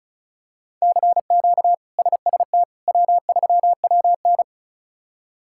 Morse code, heard as a keyed tone, about 36 wpm, 700 Hz. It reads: CQ SST W3WN
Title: K1USN SST QSO Practice - 35wpm